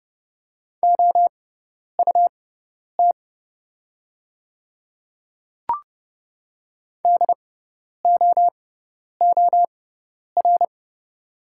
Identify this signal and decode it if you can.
Morse code, heard as a keyed tone, about 32 wpm, spaced as for 10 wpm, 700 Hz. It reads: OUT DOOR
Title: 6x Character Spacing - Top 500 Words - 30wpm